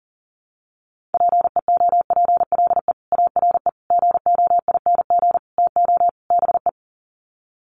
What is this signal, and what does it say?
Morse code, heard as a keyed tone, about 37 wpm, 700 Hz. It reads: PEOPLE ARE GOING TO BE
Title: Sets of 5 Words Encore 40wpm